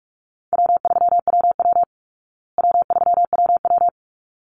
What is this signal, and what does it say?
Morse code, heard as a keyed tone, about 44 wpm, 700 Hz. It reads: W3WW W3WW
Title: Easy Call Signs Repeated 2x - 45wpm